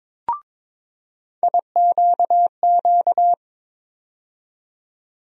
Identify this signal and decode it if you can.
Morse code, heard as a keyed tone, about 22 wpm, 700 Hz. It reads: IQQ
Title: USA Call Sign Postfixes - 3 Letters 22wpm